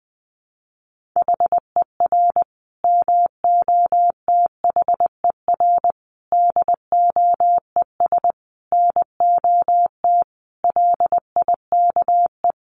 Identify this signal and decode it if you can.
Morse code, heard as a keyed tone, about 20 wpm, 700 Hz. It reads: HER MOTHER DOES NOT LIKE
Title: Sentences from Top 1400 Words 20wpm